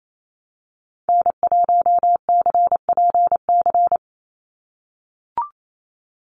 Morse code, heard as a keyed tone, about 28 wpm, 700 Hz. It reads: N1CPC
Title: USA Call Signs 28wpm